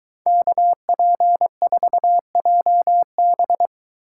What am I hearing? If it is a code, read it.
Morse code, heard as a keyed tone, about 23 wpm, 700 Hz. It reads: KP4JB